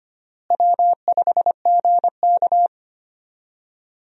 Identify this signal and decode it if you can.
Morse code, heard as a keyed tone, about 25 wpm, 700 Hz. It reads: W5GK
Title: USA Call Signs - 4 Characters - Repeated 3x - 25wpm